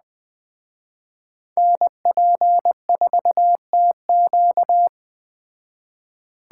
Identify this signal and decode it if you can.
Morse code, heard as a keyed tone, about 21 wpm, 700 Hz. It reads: NP4TQ